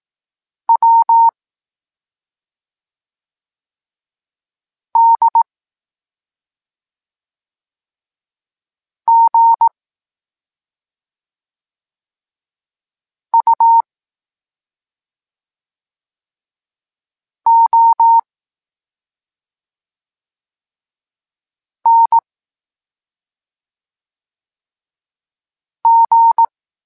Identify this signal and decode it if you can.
Morse code, heard as a keyed tone, about 18 wpm, 920 Hz. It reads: W D G U O N G
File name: Tekst-23 4